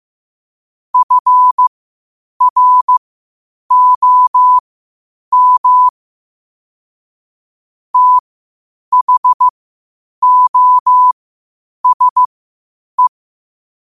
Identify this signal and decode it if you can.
Morse code, heard as a keyed tone, about 15 wpm, spaced as for 8 wpm, 1000 Hz. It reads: FROM THOSE
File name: qotdfiles / 8wpm